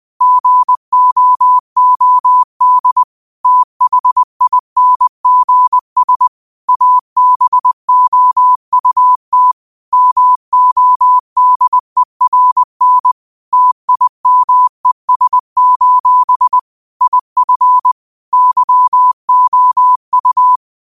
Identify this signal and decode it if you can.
Morse code, heard as a keyed tone, about 20 wpm, 1000 Hz. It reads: GOOD THINGS ABOUT MODERN TIMES: IF YOU
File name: qotdfiles / 20wpm